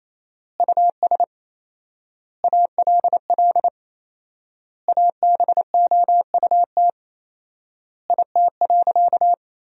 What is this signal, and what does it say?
Morse code, heard as a keyed tone, about 28 wpm, 700 Hz. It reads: US ALL ABOUT IT.
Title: 4x Word-Spacing - Sentences from Top 200 Words - 28wpm